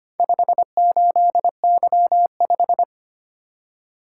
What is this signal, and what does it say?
Morse code, heard as a keyed tone, about 25 wpm, 700 Hz. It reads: 58Y5
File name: practice / 25wpm / Four Letter-Number Combinations - 25wpm